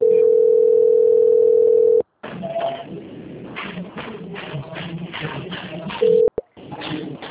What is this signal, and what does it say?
Ringback tone (North America), 440 + 480 Hz.